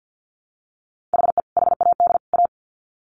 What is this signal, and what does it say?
Morse code, heard as a keyed tone, about 49 wpm, 700 Hz. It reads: HE HAD A
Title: Sets of 3 Words 50wpm